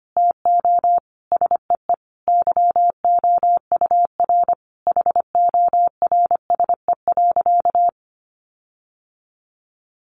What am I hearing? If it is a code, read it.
Morse code, heard as a keyed tone, about 25 wpm, 700 Hz. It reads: TO SEE YOUR HORSE.
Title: Sentences from Top 1500 Words 25wpm